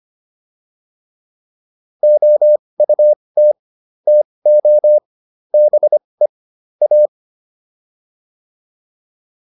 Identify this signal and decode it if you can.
Morse code, heard as a keyed tone, about 25 wpm, spaced as for 20 wpm, 600 Hz. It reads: OUT TO BE A